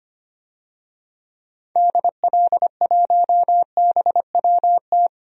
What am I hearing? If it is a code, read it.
Morse code, heard as a keyed tone, about 25 wpm, 690 Hz. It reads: DL1BWT
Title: DXCC-100 Call Signs - 6 Characters - 25wpm